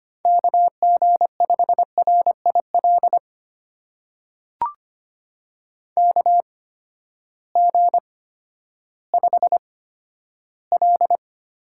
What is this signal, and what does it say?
Morse code, heard as a keyed tone, about 25 wpm, spaced as for 12 wpm, 700 Hz. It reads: KG5RIL KG5L
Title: USA Call Signs - 6 Characters - 8x Character Spacing - 25wpm